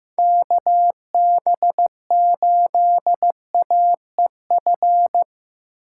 Morse code, heard as a keyed tone, about 15 wpm, 700 Hz. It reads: KB8AEF
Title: USA Call Signs - 6 Characters - 6x Character Spacing - 15wpm